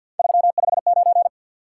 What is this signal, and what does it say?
Morse code, heard as a keyed tone, about 47 wpm, 700 Hz. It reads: 359
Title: Top 180 QSO Elements 50wpm